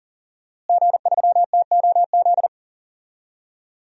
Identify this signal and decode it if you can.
Morse code, heard as a keyed tone, about 37 wpm, 700 Hz. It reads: G3TOZ
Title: DXCC-100 Call Signs - 5 Characters - Repeated 3x - 40wpm